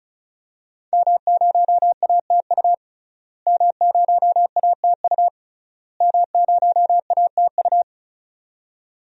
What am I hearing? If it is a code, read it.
Morse code, heard as a keyed tone, about 36 wpm, 700 Hz. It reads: M0ATU M0ATU M0ATU
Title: DXCC-100 Call Signs - 5 Characters - Repeated 3x - 35wpm